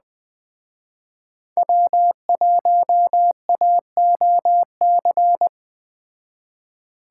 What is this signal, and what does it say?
Morse code, heard as a keyed tone, about 20 wpm, 700 Hz. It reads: W1AOC